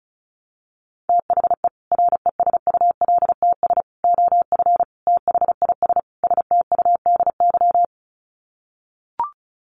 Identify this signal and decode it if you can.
Morse code, heard as a keyed tone, about 34 wpm, 700 Hz. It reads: THE RESULTS OF THIS STUDY
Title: Sets of 5 Words 35wpm